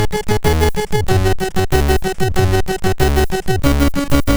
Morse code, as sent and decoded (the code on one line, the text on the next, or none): none